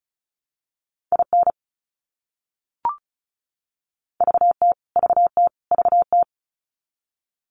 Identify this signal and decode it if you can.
Morse code, heard as a keyed tone, about 34 wpm, 700 Hz. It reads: IN VT VT VT
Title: US States Repeated 3x 35wpm